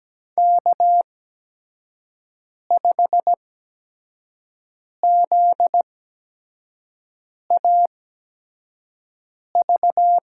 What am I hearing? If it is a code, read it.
Morse code, heard as a keyed tone, about 17 wpm, spaced as for 5 wpm, 700 Hz. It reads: K5ZAV
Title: USA Call Signs - 8x Character Spacing - 17wpm